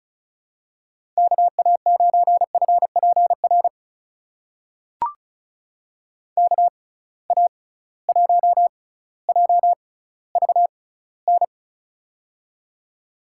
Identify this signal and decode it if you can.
Morse code, heard as a keyed tone, about 35 wpm, spaced as for 18 wpm, 700 Hz. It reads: KA9FPR KA1JVN
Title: USA Call Signs - 6 Characters - 6x Character Spacing - 35wpm